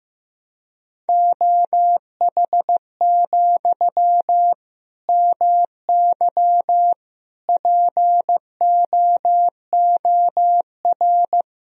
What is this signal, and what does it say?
Morse code, heard as a keyed tone, about 15 wpm, 700 Hz. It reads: OH, MY POOR
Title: Sentences from Top 1000 Words - Review Part 3 - 15wpm